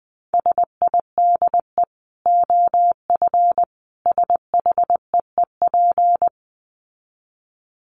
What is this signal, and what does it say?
Morse code, heard as a keyed tone, about 20 wpm, 700 Hz. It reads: SIDE OF SHEEP
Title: Dit Challenge 20wpm